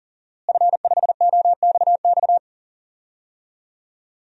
Morse code, heard as a keyed tone, about 40 wpm, 700 Hz. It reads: F5OXX
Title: DXCC-100 Call Signs - 5 Characters - Repeated 2x - 40wpm